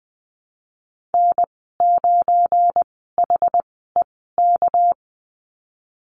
Morse code, heard as a keyed tone, about 20 wpm, spaced as for 15 wpm, 700 Hz. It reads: N9HEK
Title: 238 Lesson 38g - Letter q - Call Signs 20wpm